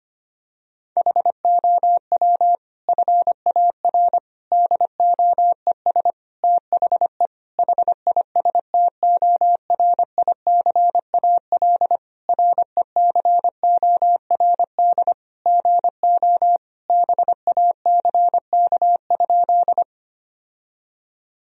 Morse code, heard as a keyed tone, about 25 wpm, 700 Hz. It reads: HOW FAR DOES THE HISTORICAL RECORD GO BACK?
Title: Sentences from Top 1000 Words - Review Part 4 - 25wpm